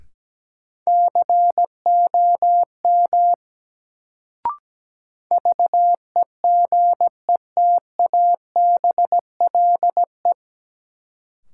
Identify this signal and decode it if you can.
Morse code, heard as a keyed tone, about 17 wpm, 700 Hz. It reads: COM VEGETABLE
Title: Sentences from Top 1300 Words - New Word Drill Alt - 17wpm